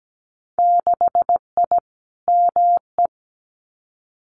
Morse code, heard as a keyed tone, about 17 wpm, 700 Hz. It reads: 6I ME
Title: Winter Field Day Exchanges Encore 17wpm